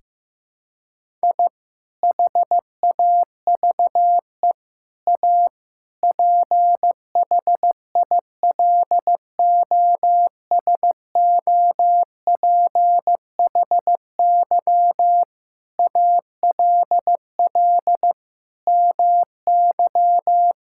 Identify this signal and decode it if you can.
Morse code, heard as a keyed tone, about 16 wpm, 700 Hz. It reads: I HAVE A PHILOSOPHY ALL MY
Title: Sentences from Top 1000 Words - Review Part 6 - 15wpm